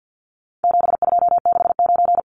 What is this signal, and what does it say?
Morse code, heard as a keyed tone, about 50 wpm, 700 Hz. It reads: 7269
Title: Numbers - 4-Digit 50wpm